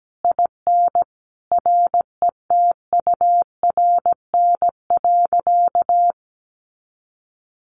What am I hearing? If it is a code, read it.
Morse code, heard as a keyed tone, about 17 wpm, 700 Hz. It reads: IN RETURN.
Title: A Sound Of Thunder 17wpm